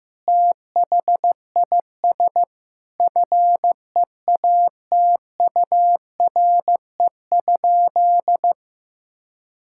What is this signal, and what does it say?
Morse code, heard as a keyed tone, about 15 wpm, 700 Hz. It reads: THIS FEATURE?
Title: Sentences from Top 1000 Words - Review Part 5 - 15wpm